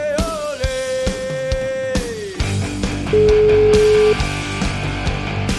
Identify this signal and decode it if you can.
Ringback tone (Europe, 425 Hz), with noise.